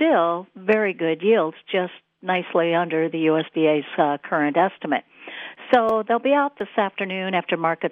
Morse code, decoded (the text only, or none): none